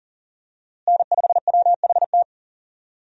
Morse code, heard as a keyed tone, about 37 wpm, 700 Hz. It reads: N5WHT
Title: USA Call Signs - 5 Characters - Repeated 2x - 40wpm